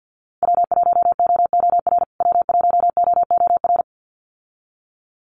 Morse code, heard as a keyed tone, about 50 wpm, 700 Hz. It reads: W1OOR W1OOR
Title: USA Call Signs Repeated 2x 50wpm